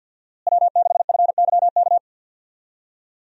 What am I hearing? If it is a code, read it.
Morse code, heard as a keyed tone, about 49 wpm, 700 Hz. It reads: W6FYK
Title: USA Call Signs - 5 Characters - Repeated 2x - 50wpm